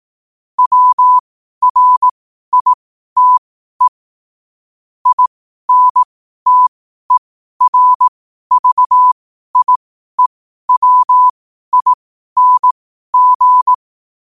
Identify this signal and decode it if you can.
Morse code, heard as a keyed tone, about 18 wpm, spaced as for 12 wpm, 1000 Hz. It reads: WRITE INTERVIEWING
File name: qotdfiles / 12wpm